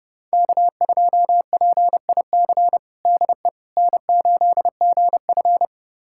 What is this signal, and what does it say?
Morse code, heard as a keyed tone, about 30 wpm, 700 Hz. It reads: K2PIC DE N8GF
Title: USA Callsign DE Callsign 30wpm